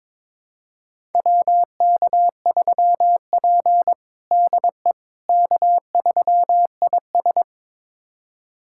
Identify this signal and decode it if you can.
Morse code, heard as a keyed tone, about 22 wpm, 700 Hz. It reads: WK3P DE K3IS
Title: USA Callsign DE Callsign - 4 Characters - 22wpm